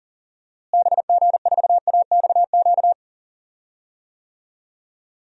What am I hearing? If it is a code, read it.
Morse code, heard as a keyed tone, about 40 wpm, 700 Hz. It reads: BG4AXQ
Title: DXCC-100 Call Signs - 6 Characters - Repeated 2x - 40wpm